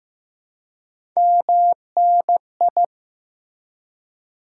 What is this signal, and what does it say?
Morse code, heard as a keyed tone, about 15 wpm, 700 Hz. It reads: MNI
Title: Top 90 QSO Elements 15wpm